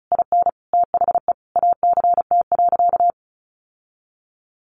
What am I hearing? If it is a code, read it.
Morse code, heard as a keyed tone, about 34 wpm, 700 Hz. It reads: IN THE ACT.
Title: Sentences from Top 1100 Words 35wpm